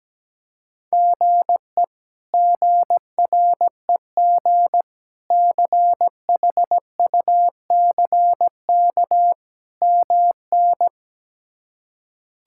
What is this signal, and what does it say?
Morse code, heard as a keyed tone, about 17 wpm, 700 Hz. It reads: GE GREG CHUCK MN
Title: K1USN SST QSO Practice - 17wpm